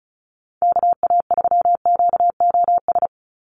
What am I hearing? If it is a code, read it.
Morse code, heard as a keyed tone, about 34 wpm, 700 Hz. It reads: KA3QOS
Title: USA Call Signs 35wpm